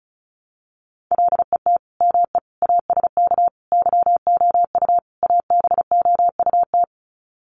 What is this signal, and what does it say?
Morse code, heard as a keyed tone, about 34 wpm, 700 Hz. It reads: LET ME ASK YOU ABOUT
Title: Sets of 5 Words 35wpm